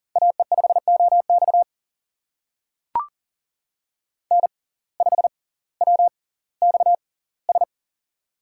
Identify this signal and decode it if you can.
Morse code, heard as a keyed tone, about 40 wpm, spaced as for 20 wpm, 700 Hz. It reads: AE5OX N5WXS
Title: USA Call Signs - 5 Characters - 6x Character Spacing - 40wpm